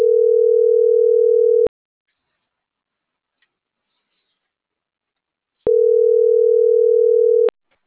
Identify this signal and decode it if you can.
Ringback tone (North America), 440 + 480 Hz.